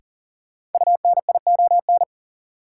Morse code, heard as a keyed tone, about 40 wpm, 700 Hz. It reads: UNION
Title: Sentences from Top 1400 Words - New Word Drill - 40wpm